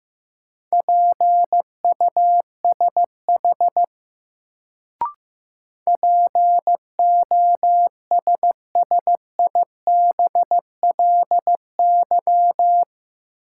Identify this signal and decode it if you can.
Morse code, heard as a keyed tone, about 15 wpm, 700 Hz. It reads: PUSH POSSIBLY
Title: Sentences from Top 1300 Words - New Word Drill Alt - 15wpm